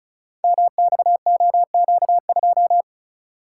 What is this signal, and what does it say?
Morse code, heard as a keyed tone, about 36 wpm, 700 Hz. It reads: MXOQ2
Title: Code Groups - 5 Letter:Numbers - 35wpm